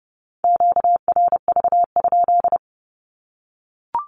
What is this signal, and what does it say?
Morse code, heard as a keyed tone, about 30 wpm, 700 Hz. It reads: QRV?
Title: Q-Codes 30wpm